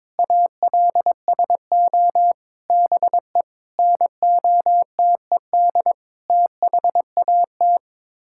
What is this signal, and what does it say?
Morse code, heard as a keyed tone, about 21 wpm, 700 Hz. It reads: ALSO BE NOTED THAT
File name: practice / 22wpm / Sets of 6 Words - 22wpm